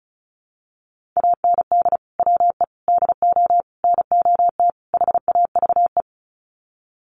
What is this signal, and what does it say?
Morse code, heard as a keyed tone, about 34 wpm, 700 Hz. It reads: AND WE DO NOT HAVE
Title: Sets of 5 Words Encore 35wpm